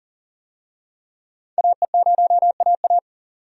Morse code, heard as a keyed tone, about 40 wpm, 700 Hz. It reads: AE0AA
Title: USA Call Signs - 5 Characters - 40wpm